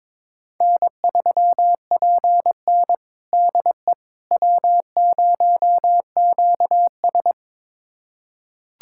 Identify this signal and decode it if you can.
Morse code, heard as a keyed tone, about 22 wpm, 700 Hz. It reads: N3PN DE W0QS
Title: USA Callsign DE Callsign - 4 Characters - 22wpm